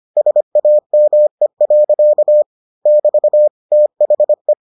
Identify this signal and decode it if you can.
Morse code, heard as a keyed tone, about 25 wpm, 600 Hz. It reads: SAME. <BT> THE